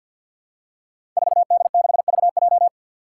Morse code, heard as a keyed tone, about 52 wpm, 700 Hz. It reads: 4D6VJ
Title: Five Challenging Letter-Number Combinations 50wpm